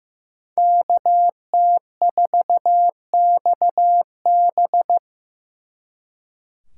Morse code, heard as a keyed tone, about 18 wpm, 700 Hz. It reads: KT4XB
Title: Speed Racing - USA 2x2 Call Signs 15wpm